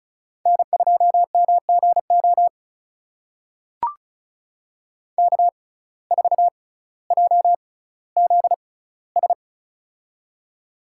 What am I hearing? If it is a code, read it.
Morse code, heard as a keyed tone, about 36 wpm, spaced as for 18 wpm, 700 Hz. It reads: N2MGO K4JZS
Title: USA Call Signs - 5 Characters - 6x Character Spacing - 35wpm